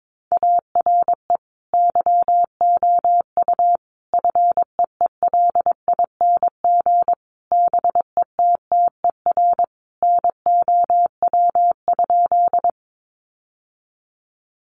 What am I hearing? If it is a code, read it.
Morse code, heard as a keyed tone, about 22 wpm, 700 Hz. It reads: ARE YOU FEELING BETTER NOW?
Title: Sentences from Top 1200 Words 22wpm